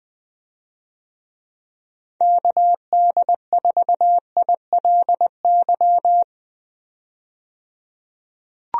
Morse code, heard as a keyed tone, about 21 wpm, 700 Hz. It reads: KD4ILY
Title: USA Call Signs - 6 Characters - 20wpm